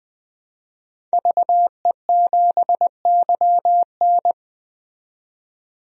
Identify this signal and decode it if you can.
Morse code, heard as a keyed tone, about 21 wpm, 700 Hz. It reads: VE7YN
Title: DXCC-100 Call Signs - 5 Characters - 20wpm